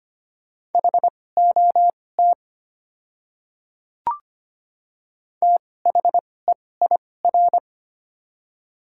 Morse code, heard as a keyed tone, about 25 wpm, spaced as for 18 wpm, 690 Hz. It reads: HOT THEIR
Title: 2x Character Spacing - Top 300 Words - 25wpm